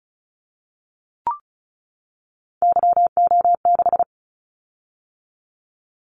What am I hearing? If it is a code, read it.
Morse code, heard as a keyed tone, about 36 wpm, 700 Hz. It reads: YO6
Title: Callsign Prefixes for Top 25 DXCC - 35wpm